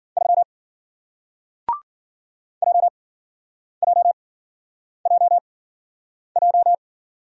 Morse code, heard as a keyed tone, about 54 wpm, spaced as for 9 wpm, 700 Hz. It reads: 3 JJJJ